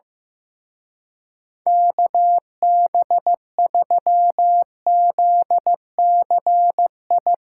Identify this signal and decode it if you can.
Morse code, heard as a keyed tone, about 16 wpm, 700 Hz. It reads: KB3ZCI
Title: Speed Racing - USA 2x3 Call Signs 15wpm